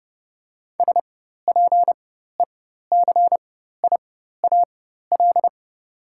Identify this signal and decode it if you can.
Morse code, heard as a keyed tone, about 30 wpm, spaced as for 14 wpm, 700 Hz. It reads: SPECIAL
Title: 4x Character Spacing - Top 500 Words - 30wpm